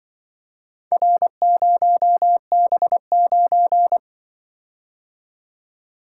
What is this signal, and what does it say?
Morse code, heard as a keyed tone, about 24 wpm, 700 Hz. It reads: R0B9
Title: Speed-Racing - Four Letter-Number Combinations - 20wpm